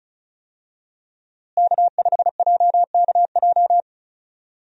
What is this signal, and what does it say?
Morse code, heard as a keyed tone, about 36 wpm, 700 Hz. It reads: K5JKJ
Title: USA Call Signs - 5 Characters - Repeated 3x - 35wpm